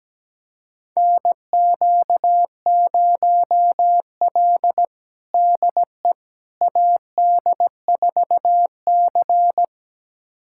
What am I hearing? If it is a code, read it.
Morse code, heard as a keyed tone, about 17 wpm, 700 Hz. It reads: NQ0L DE AD4C
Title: USA Callsign DE Callsign - 4 Characters - 17wpm